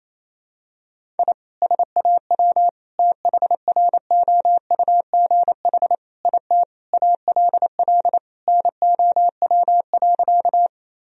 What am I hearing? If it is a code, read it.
Morse code, heard as a keyed tone, about 28 wpm, 700 Hz. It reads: I SAW THROUGH IT ALL NOW.
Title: Sentences from Top 1000 Words - Review Part 1 - 28wpm